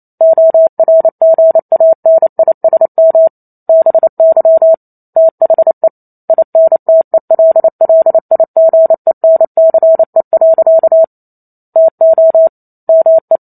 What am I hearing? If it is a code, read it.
Morse code, heard as a keyed tone, about 28 wpm, 650 Hz. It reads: ORGANISM BY THE INTELLIGENCE. TO ME